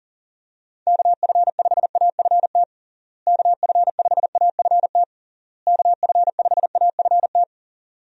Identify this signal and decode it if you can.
Morse code, heard as a keyed tone, about 40 wpm, 700 Hz. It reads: KF5AFT KF5AFT KF5AFT
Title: USA Call Signs - 6 Characters - Repeated 3x - 40wpm